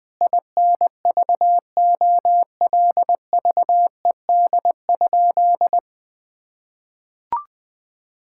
Morse code, heard as a keyed tone, about 20 wpm, 700 Hz. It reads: INVOLVED?
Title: Sentences from Top 1000 Words - Review Part 5 - 20wpm